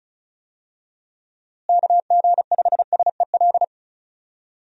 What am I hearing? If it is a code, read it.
Morse code, heard as a keyed tone, about 36 wpm, 700 Hz. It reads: KG5SEL
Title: USA Call Signs - 6 Characters - Repeated 3x - 35wpm